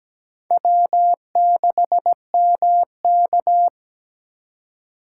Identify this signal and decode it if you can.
Morse code, heard as a keyed tone, about 17 wpm, 700 Hz. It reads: W6MK
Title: USA Call Signs - 4 Characters - 2x Character Spacing - 17wpm